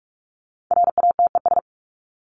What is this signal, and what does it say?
Morse code, heard as a keyed tone, about 44 wpm, 700 Hz. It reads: RATES
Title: ES:S Suffix - Top 100 Words with Base Words 45wpm